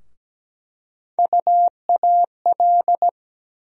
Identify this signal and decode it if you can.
Morse code, heard as a keyed tone, about 17 wpm, 700 Hz. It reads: UAL
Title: USA Call Sign Postfixes - 3 Letters 17wpm